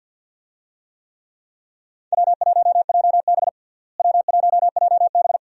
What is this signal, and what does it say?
Morse code, heard as a keyed tone, about 51 wpm, 700 Hz. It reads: W1JB W1JB
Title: USA Call Signs - 4 Characters - Repeated 2x - 50wpm